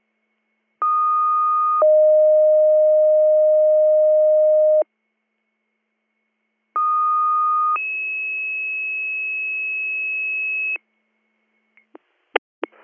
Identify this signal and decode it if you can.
Two-tone page, heard as a one-second tone, then a three-second tone, about 1230 then 620 Hz, about 1230 then 2600 Hz. It is heard from 0.8 to 10.8 s.